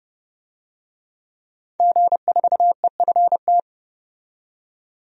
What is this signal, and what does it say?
Morse code, heard as a keyed tone, about 30 wpm, 700 Hz. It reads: G4EFT
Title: DXCC-100 Call Signs - 5 Characters - 30wpm